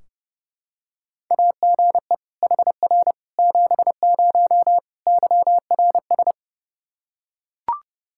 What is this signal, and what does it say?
Morse code, heard as a keyed tone, about 30 wpm, 700 Hz. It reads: AGE HR 70 YRS
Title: Practice QSO - Exchange 3 - More Info - 30wpm